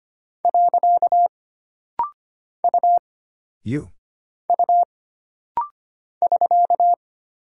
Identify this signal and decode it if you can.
Morse code, heard as a keyed tone, about 25 wpm, 700 Hz. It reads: . U U <SK>
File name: Single Character (Letter:Number:Punctuation:Prosign) - Rapid-Fire - 25wpm